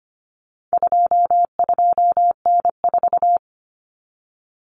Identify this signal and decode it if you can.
Morse code, heard as a keyed tone, about 25 wpm, 700 Hz. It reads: 22N4
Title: Common Cut Numbers - 4-Digit 25wpm